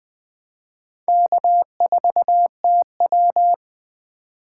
Morse code, heard as a keyed tone, about 20 wpm, 700 Hz. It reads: K4TW